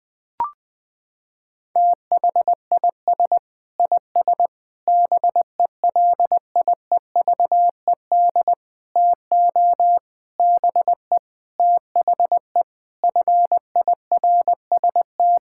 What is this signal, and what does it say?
Morse code, heard as a keyed tone, about 21 wpm, 700 Hz. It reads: THIS IS BELIEVED TO BE THE FIRST
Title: Sentences with 6-Word Phrases - 20wpm